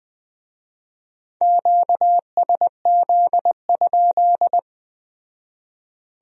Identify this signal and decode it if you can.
Morse code, heard as a keyed tone, about 20 wpm, 700 Hz. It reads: QSZ?